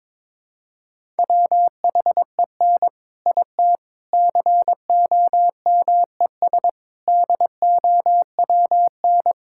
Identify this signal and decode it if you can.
Morse code, heard as a keyed tone, about 22 wpm, 700 Hz. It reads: WHEN IT COMES DOWN
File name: practice / 22wpm / Sets of 6 Words - 22wpm